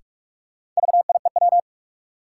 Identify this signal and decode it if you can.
Morse code, heard as a keyed tone, about 45 wpm, 700 Hz. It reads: VIEW
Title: Sentences from Top 900 Words - New Word Drill - 45wpm